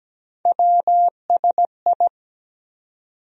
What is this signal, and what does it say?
Morse code, heard as a keyed tone, about 17 wpm, 700 Hz. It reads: WSI
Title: USA Call Sign Postfixes - 3 Letters 17wpm